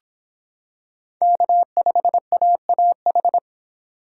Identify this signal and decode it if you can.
Morse code, heard as a keyed tone, about 26 wpm, 700 Hz. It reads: K5AAH